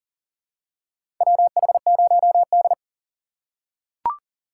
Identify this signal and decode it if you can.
Morse code, heard as a keyed tone, about 40 wpm, 700 Hz. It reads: WH0D
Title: USA Call Signs - 4 Characters - 3x Character Spacing - 40wpm